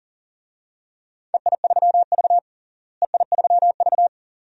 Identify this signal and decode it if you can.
Morse code, heard as a keyed tone, about 40 wpm, 700 Hz. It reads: EI3V EI3V
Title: DXCC-100 Call Signs - 4 Characters - Repeated 2x - 40wpm